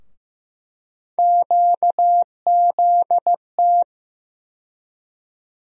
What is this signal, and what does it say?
Morse code, heard as a keyed tone, about 15 wpm, 700 Hz. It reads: QZT
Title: USA Call Sign Postfixes - 3 Letters 15wpm